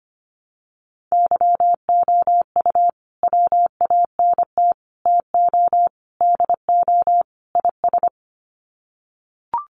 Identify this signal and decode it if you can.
Morse code, heard as a keyed tone, about 25 wpm, 700 Hz. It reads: YOU WANT TO DO IS
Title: Sets of 5 Words Encore2 25wpm